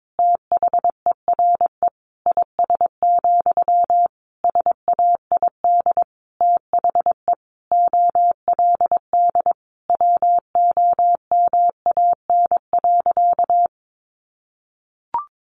Morse code, heard as a keyed tone, about 22 wpm, 700 Hz. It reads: THERE IS, SAID THE OLD WOMAN.
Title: Sentences from Top 1100 Words 22wpm